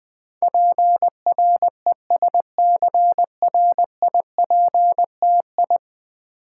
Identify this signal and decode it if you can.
Morse code, heard as a keyed tone, about 20 wpm, 700 Hz. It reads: PRESCRIPTI
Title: Build A Word - 5 Letters At A Time - 12-Letter Words - 20wpm